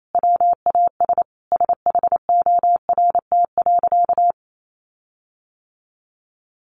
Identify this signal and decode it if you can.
Morse code, heard as a keyed tone, about 28 wpm, 700 Hz. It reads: WAS SHORT.
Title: Sentences from Top 1300 Words 28wpm